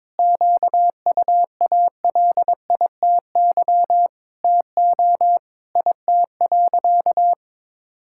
Morse code, heard as a keyed tone, about 22 wpm, 700 Hz. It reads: QUALITY TO IT.
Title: Sentences from Top 1000 Words - Review Part 6 - 22wpm